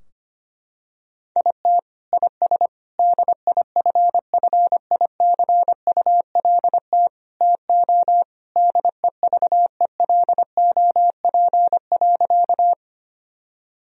Morse code, heard as a keyed tone, about 25 wpm, 700 Hz. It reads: IT IS DIFFICULT TO DEVELOP.
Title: Sentences from Top 1000 Words - Review Part 2 - 25wpm